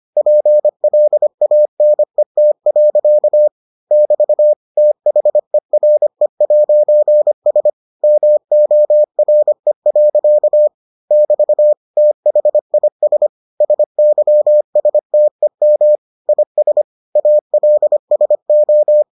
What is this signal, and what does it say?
Morse code, heard as a keyed tone, about 25 wpm, 600 Hz. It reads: PLANET. <BT> THERE'S MORE. <BT> THIS SYSTEM IS ALSO